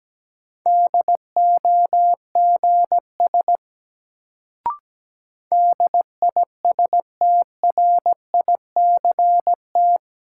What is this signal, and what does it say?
Morse code, heard as a keyed tone, about 17 wpm, 700 Hz. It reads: DOGS DISTRICT
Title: Sentences from Top 1600 Words - New Word Drill Alt - 17wpm